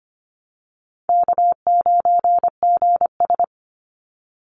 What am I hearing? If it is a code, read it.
Morse code, heard as a keyed tone, about 25 wpm, 700 Hz. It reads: K9GS
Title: Call Signs Repeated 3x - 25wpm